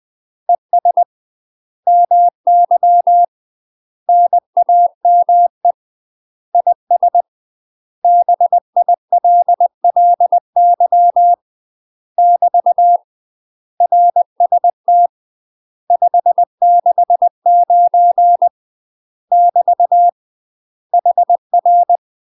Morse code, heard as a keyed tone, about 20 wpm, 700 Hz. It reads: ES MY NAME IS BILLY <BT> RST 569 <BT> HR
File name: UK MASTER QSOs TRAINING / 20WPM / UK MASTER QSOs 0050